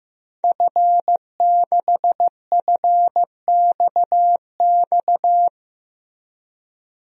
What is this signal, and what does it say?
Morse code, heard as a keyed tone, about 15 wpm, 700 Hz. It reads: F6FXX
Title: DXCC-100 Call Signs - 5 Characters - Repeated 3x - 15wpm